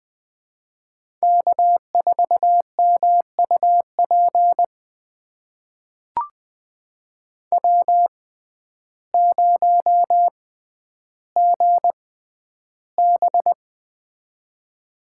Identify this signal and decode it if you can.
Morse code, heard as a keyed tone, about 20 wpm, spaced as for 11 wpm, 700 Hz. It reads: K4MUP W0GB
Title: USA Call Signs - 6x Character Spacing - 20wpm